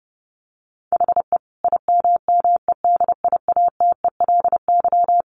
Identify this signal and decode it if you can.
Morse code, heard as a keyed tone, about 30 wpm, 700 Hz. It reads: HE IMMEDIATELY
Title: Sentences from Top 1100 Words 30wpm